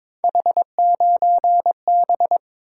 Morse code, heard as a keyed tone, about 22 wpm, 700 Hz. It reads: H9B
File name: practice / 22wpm / Three Letter-Number Combinations - 22wpm